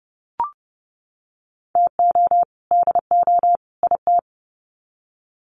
Morse code, heard as a keyed tone, about 30 wpm, 700 Hz. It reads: TO DO IT
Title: Sets of 3 Words 30wpm